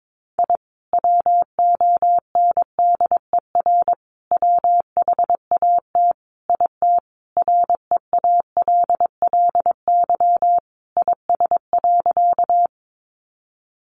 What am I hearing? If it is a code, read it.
Morse code, heard as a keyed tone, about 22 wpm, 700 Hz. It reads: I WONDER WHAT IT REALLY IS.
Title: Sentences from Top 600 Words 22wpm